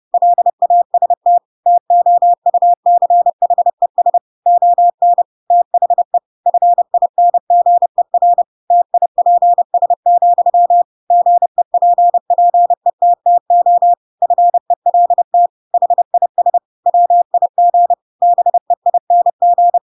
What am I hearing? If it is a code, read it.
Morse code, heard as a keyed tone, about 30 wpm, 700 Hz. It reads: LAST TOUCHES ON THE FINGER TIPS, GEPPETTO FELT HIS WIG BEING
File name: Pinocchio / Pinocchio ch3 30 30 ik1hge